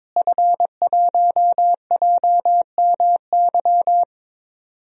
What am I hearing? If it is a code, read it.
Morse code, heard as a keyed tone, about 22 wpm, 700 Hz. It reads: F1JMY
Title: DXCC-100 Call Signs - 5 Characters - Repeated 2x - 22wpm